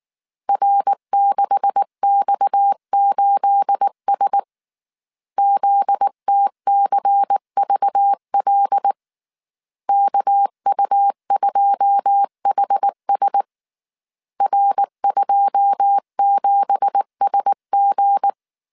Morse code, heard as a keyed tone, about 19 wpm, 800 Hz. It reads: R6X8S ZTCVL KU2HS R27SG
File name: CW / DET / 1 9 1500C S A 18wpm 2459